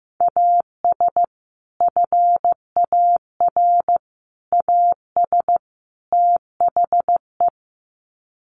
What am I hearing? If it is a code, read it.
Morse code, heard as a keyed tone, about 15 wpm, 700 Hz. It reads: AS FAR AS THE
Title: Sets of 4 Words 15wpm